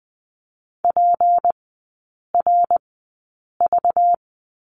Morse code, heard as a keyed tone, about 20 wpm, spaced as for 8 wpm, 690 Hz. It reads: PRV
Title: 191 Lesson 31e - Letter v - Three Characters 20wpm